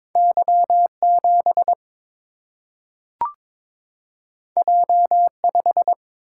Morse code, heard as a keyed tone, about 22 wpm, 700 Hz. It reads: Y 7 J 5
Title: Single-Letter Single-Number 22wpm